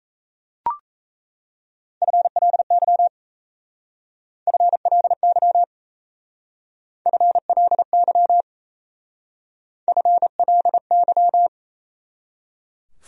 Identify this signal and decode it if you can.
Morse code, heard as a keyed tone, about 37 wpm, 700 Hz. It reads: FLY FLY FLY FLY